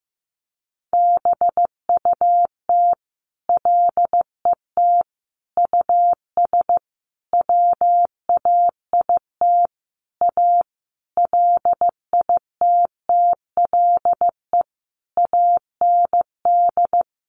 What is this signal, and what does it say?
Morse code, heard as a keyed tone, about 15 wpm, 700 Hz. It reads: BUT LET US WAIT A LITTLE AND
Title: Sentences from Top 1500 Words 15wpm